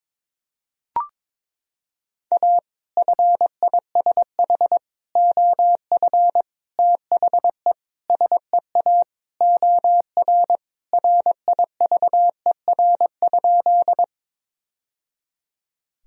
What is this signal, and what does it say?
Morse code, heard as a keyed tone, about 22 wpm, 700 Hz. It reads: A FISH OF THE SEA OR RIVER?
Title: Sentences from Top 1000 Words - Review Part 1 - 22wpm